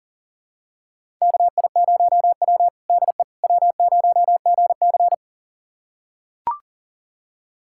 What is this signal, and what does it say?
Morse code, heard as a keyed tone, about 40 wpm, 700 Hz. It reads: KI0W DE W0GC
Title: USA Callsign DE Callsign - 4 Characters - 40wpm